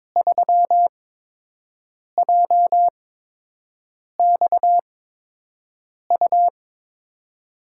Morse code, heard as a keyed tone, about 22 wpm, spaced as for 6 wpm, 700 Hz. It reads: 3JXU
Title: USA Call Signs - 5 Characters - 8x Character Spacing - 22wpm